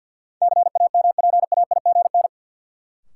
Morse code, heard as a keyed tone, about 49 wpm, 700 Hz. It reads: CAMPAIGN
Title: Sentences from Top 1200 Words - New Word Drill - 50wpm